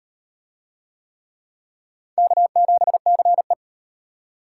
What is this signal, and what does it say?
Morse code, heard as a keyed tone, about 37 wpm, 700 Hz. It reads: K7CE